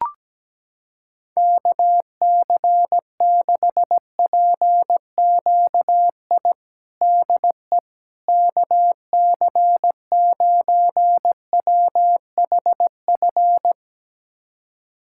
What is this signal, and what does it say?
Morse code, heard as a keyed tone, about 17 wpm, 700 Hz. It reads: KC6PQI DE KC9WHF
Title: USA Callsign DE Callsign - 6 Characters - 17wpm